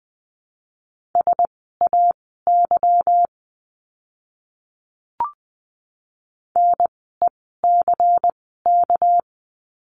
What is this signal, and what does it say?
Morse code, heard as a keyed tone, about 19 wpm, spaced as for 14 wpm, 700 Hz. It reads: SAY NECK
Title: 199 Lesson 32f - Letter k - Words 20wpm